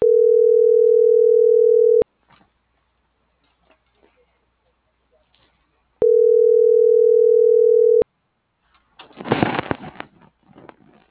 Ringback tone (North America), 440 + 480 Hz.